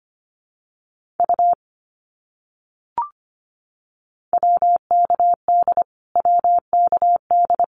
Morse code, heard as a keyed tone, about 25 wpm, 700 Hz. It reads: U WKD WKD
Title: Abbreviations Repeated 3x - 25wpm